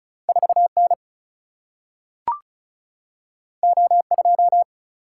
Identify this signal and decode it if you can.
Morse code, heard as a keyed tone, about 36 wpm, 700 Hz. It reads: 4 N O 2
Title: Single-Letter Single-Number 35wpm